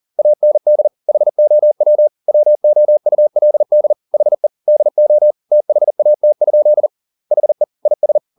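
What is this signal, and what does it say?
Morse code, heard as a keyed tone, about 40 wpm, 600 Hz. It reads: AND HOW WOULD HE DO THAT? HE IS
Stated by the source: Morse code at 40 wpm, practice text, CW med Gnister.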